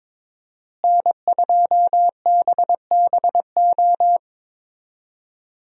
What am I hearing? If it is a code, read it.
Morse code, heard as a keyed tone, about 22 wpm, 700 Hz. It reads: N2BBO
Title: USA Call Signs - 5 Characters - Repeated 3x - 22wpm